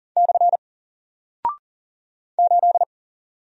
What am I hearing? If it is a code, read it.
Morse code, heard as a keyed tone, about 39 wpm, 700 Hz. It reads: / 8
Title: Single Character (Letter:Number:Punctuation:Prosign) 40wpm